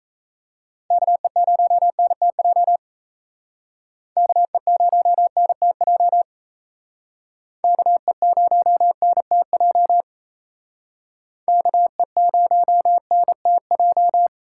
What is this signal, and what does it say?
Morse code, heard as a keyed tone, about 36 wpm, 700 Hz. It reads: KE0NTJ KE0NTJ KE0NTJ KE0NTJ